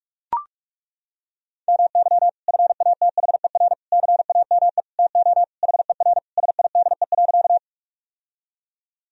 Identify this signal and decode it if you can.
Morse code, heard as a keyed tone, about 42 wpm, 700 Hz. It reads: MY FATHER CAME TO HER SIDE.
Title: Sentences from Top 1000 Words - Review Part 1 - 45wpm